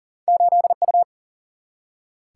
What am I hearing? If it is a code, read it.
Morse code, heard as a keyed tone, about 40 wpm, 700 Hz. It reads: 8U
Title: Single-Letter Single-Number 40wpm